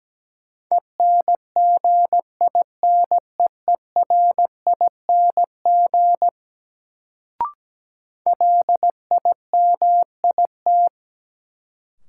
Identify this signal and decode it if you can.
Morse code, heard as a keyed tone, about 17 wpm, 700 Hz. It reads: ENGINEERING LIMIT
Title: Sentences from Top 1500 Words - New Word Drill - 17wpm